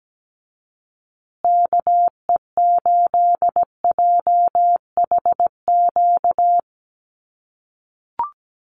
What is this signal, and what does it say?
Morse code, heard as a keyed tone, about 17 wpm, 700 Hz. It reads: KE8JHQ
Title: USA Call Signs Repeated 2x 17wpm